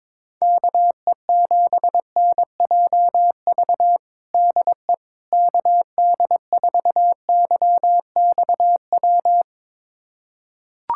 Morse code, heard as a keyed tone, about 22 wpm, 700 Hz. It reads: KE7NJV DE KD4YXW
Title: USA Callsign DE Callsign 22wpm